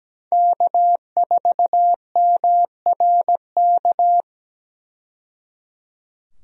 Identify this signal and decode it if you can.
Morse code, heard as a keyed tone, about 17 wpm, 700 Hz. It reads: K4MRK
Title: USA Call Signs - 5 Characters - 17wpm